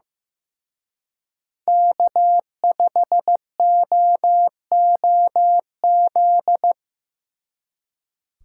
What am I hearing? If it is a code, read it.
Morse code, heard as a keyed tone, about 18 wpm, 700 Hz. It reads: K5OOZ
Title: Speed Racing - USA 1x3 Call Signs 15wpm